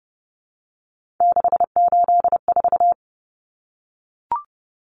Morse code, heard as a keyed tone, about 30 wpm, 700 Hz. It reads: 684
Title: Numbers - 3-Digit 30wpm